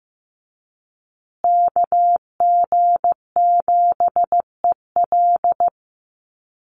Morse code, heard as a keyed tone, about 15 wpm, 700 Hz. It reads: KG7EL
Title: USA Call Signs Repeated 3x 15wpm